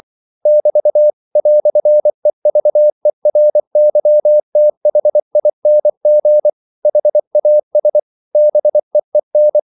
Morse code, heard as a keyed tone, about 24 wpm, 600 Hz. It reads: <BT> "EVERYTHING HAS BEEN
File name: CW / Practice / EnglishQuotations0324